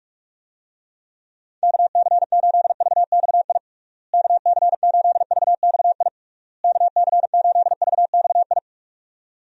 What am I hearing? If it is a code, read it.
Morse code, heard as a keyed tone, about 42 wpm, 700 Hz. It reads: KC8VXI KC8VXI KC8VXI
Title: USA Call Signs - 6 Characters - Repeated 3x - 45wpm